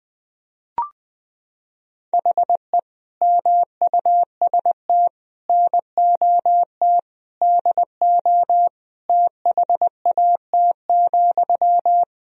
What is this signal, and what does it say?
Morse code, heard as a keyed tone, about 20 wpm, 700 Hz. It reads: HE MUST NOT DO THAT,
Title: Sentences from Top 1000 Words - Review Part 2 - 20wpm